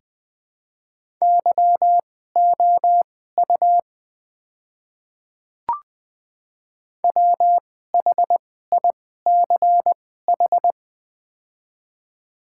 Morse code, heard as a keyed tone, about 21 wpm, spaced as for 15 wpm, 700 Hz. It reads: YOU WHICH
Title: 2x Character Spacing - Top 200 Words - 20wpm